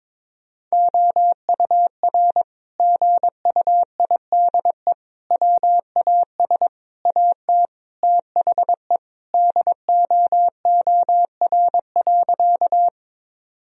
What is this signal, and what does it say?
Morse code, heard as a keyed tone, about 22 wpm, 700 Hz. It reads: OUR GUIDE WAS AT THE DOOR.
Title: Sentences from Top 1000 Words - Review Part 3 - 22wpm